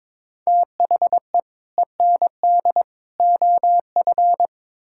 Morse code, heard as a keyed tone, about 22 wpm, 700 Hz. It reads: THE END OF
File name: practice / 22wpm / Build A Phrase - 5-Word Phrases - 22wpm